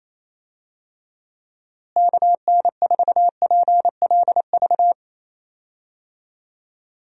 Morse code, heard as a keyed tone, about 28 wpm, 700 Hz. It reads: KN4PLV
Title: USA Call Signs - 6 Characters - Repeated 3x - 28wpm